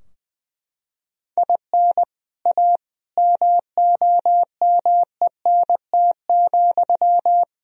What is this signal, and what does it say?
Morse code, heard as a keyed tone, about 21 wpm, 700 Hz. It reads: IN A MOMENT,
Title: Sentences from Top 1000 Words - Review Part 3 - 20wpm